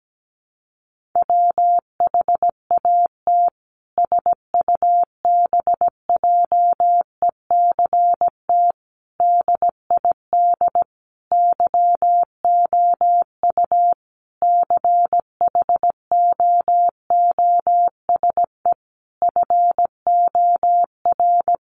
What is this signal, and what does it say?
Morse code, heard as a keyed tone, about 17 wpm, 700 Hz. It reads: WHAT SUBJECT DID YOU CHOOSE FOR
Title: Sentences from Top 1500 Words 17wpm